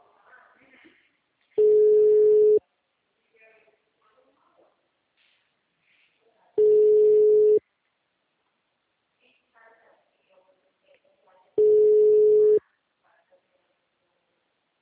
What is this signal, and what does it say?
Ringback tone (Europe, 425 Hz), with speech.
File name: out- -2015- - - .1291